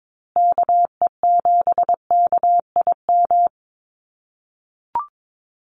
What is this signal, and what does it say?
Morse code, heard as a keyed tone, about 22 wpm, 700 Hz. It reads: KE7KIM
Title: USA Call Signs Repeated 3x 22wpm